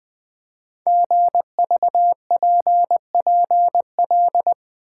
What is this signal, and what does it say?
Morse code, heard as a keyed tone, about 20 wpm, 700 Hz. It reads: GVPPL
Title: Code Groups - 5 Letters - 20wpm